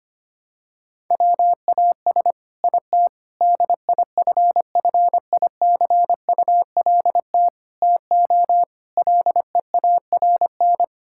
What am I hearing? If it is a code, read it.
Morse code, heard as a keyed tone, about 25 wpm, 700 Hz. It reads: WAS IT DIFFICULT TO LEARN
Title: Sentences from Top 1000 Words - Review Part 6 - 25wpm